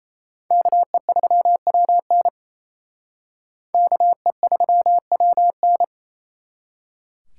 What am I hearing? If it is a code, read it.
Morse code, heard as a keyed tone, about 35 wpm, 700 Hz. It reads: KE3WN KE3WN